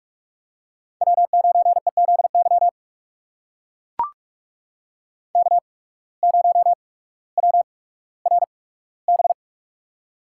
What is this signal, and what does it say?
Morse code, heard as a keyed tone, about 42 wpm, spaced as for 19 wpm, 700 Hz. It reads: W9EZY K0WRB
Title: USA Call Signs - 5 Characters - 8x Character Spacing - 45wpm